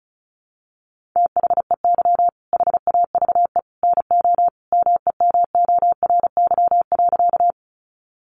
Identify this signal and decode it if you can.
Morse code, heard as a keyed tone, about 34 wpm, 700 Hz. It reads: THEY HAVE NO MEMORY.
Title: Sentences from Top 1300 Words 35wpm